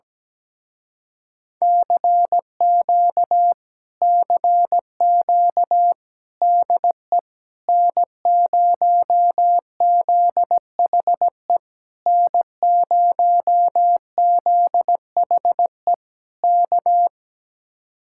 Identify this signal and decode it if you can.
Morse code, heard as a keyed tone, about 17 wpm, 700 Hz. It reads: CQ CQ DE N0ZHE N0ZHE K
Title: Speed-Racing Common QSO Phrases Encore - 17wpm